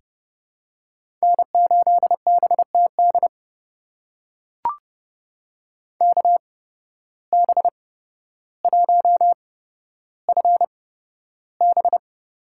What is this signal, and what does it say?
Morse code, heard as a keyed tone, about 30 wpm, spaced as for 13 wpm, 700 Hz. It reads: N8BTD KB1FB
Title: USA Call Signs - 5 Characters - 8x Character Spacing - 30wpm